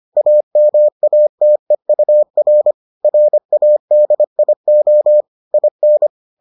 Morse code, heard as a keyed tone, about 25 wpm, 600 Hz. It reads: AMATEUR RADIO IN